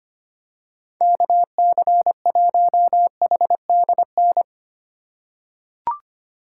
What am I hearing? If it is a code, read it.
Morse code, heard as a keyed tone, about 25 wpm, 700 Hz. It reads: KC1HDN
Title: USA Call Signs - 6 Characters - 4x Character Spacing - 25wpm